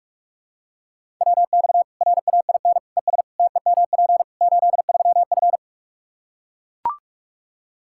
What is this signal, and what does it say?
Morse code, heard as a keyed tone, about 42 wpm, 700 Hz. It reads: WX RAIN ES TEMP 83F
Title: Practice QSO - Exchange 2 - Info - 45wpm